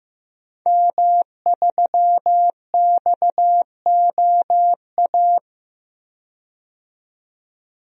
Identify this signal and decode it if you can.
Morse code, heard as a keyed tone, about 15 wpm, 700 Hz. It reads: M3XOA
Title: DXCC-100 Call Signs - 5 Characters - 15wpm